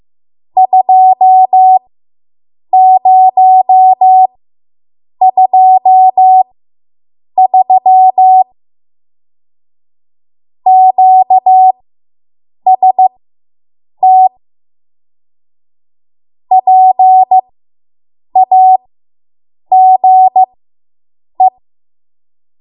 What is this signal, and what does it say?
Morse code, heard as a keyed tone, about 15 wpm, spaced as for 7 wpm, 750 Hz. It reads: 2023 QST PAGE